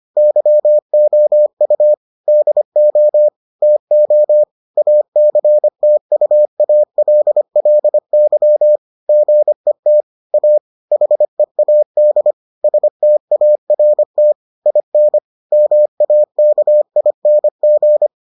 Morse code, heard as a keyed tone, about 25 wpm, 600 Hz. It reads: YOU DO TO ACTUALLY GET A HEAD START IN MAKING